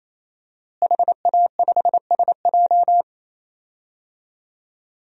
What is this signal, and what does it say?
Morse code, heard as a keyed tone, about 28 wpm, 700 Hz. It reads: HA5SJ
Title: DXCC-100 Call Signs - 5 Characters - Repeated 3x - 28wpm